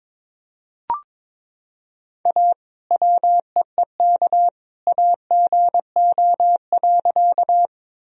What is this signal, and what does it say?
Morse code, heard as a keyed tone, about 21 wpm, 700 Hz. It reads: A WEEK AGO.
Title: Sentences from Top 1000 Words - Review Part 3 - 22wpm